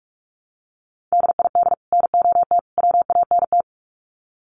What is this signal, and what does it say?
Morse code, heard as a keyed tone, about 44 wpm, 700 Hz. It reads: DID NOT WANT
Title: Sets of 3 Words 45wpm